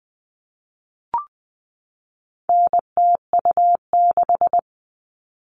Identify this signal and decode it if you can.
Morse code, heard as a keyed tone, about 20 wpm, 700 Hz. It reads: NTU6
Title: All Cut Numbers - 4-Digit 20wpm